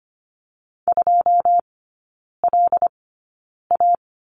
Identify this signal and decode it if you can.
Morse code, heard as a keyed tone, about 25 wpm, spaced as for 9 wpm, 700 Hz. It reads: 2LA
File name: course / 25:10wpm / 093 Lesson 16e - Letter c - Three Characters 25wpm